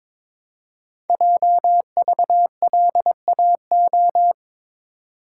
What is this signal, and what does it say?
Morse code, heard as a keyed tone, about 23 wpm, 700 Hz. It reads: JVLAO
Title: Code Groups - 5 Letter:Numbers - 22wpm